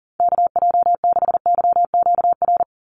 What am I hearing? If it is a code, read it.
Morse code, heard as a keyed tone, about 40 wpm, 700 Hz. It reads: KJ6YQR
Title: USA Call Signs 40wpm